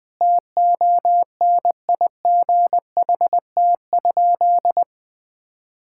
Morse code, heard as a keyed tone, about 20 wpm, 700 Hz. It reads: TONIGHT?
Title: Sentences with 4-Word Phrases - 20wpm